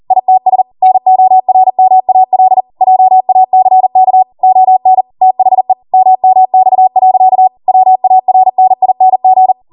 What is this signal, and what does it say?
Morse code, heard as a keyed tone, about 40 wpm, 750 Hz. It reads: ITS NORMAL JACK ON THE MMX. WARNING